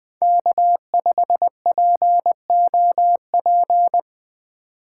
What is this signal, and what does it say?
Morse code, heard as a keyed tone, about 20 wpm, 700 Hz. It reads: K5POP
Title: USA Call Signs - 2x Character Spacing - 20wpm